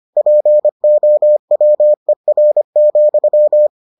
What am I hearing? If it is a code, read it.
Morse code, heard as a keyed tone, about 25 wpm, 600 Hz. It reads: POWER,